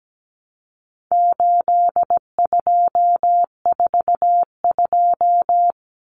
Morse code, heard as a keyed tone, about 17 wpm, 700 Hz. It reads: 8242
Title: Common Cut Numbers - 4-Digit 17wpm